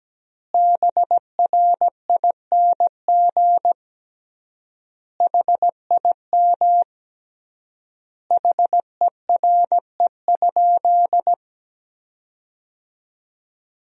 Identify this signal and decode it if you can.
Morse code, heard as a keyed tone, about 17 wpm, 700 Hz. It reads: BRING HIM HERE?
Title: 3x Word-Spacing - Sentences from Top 500 Words - 17wpm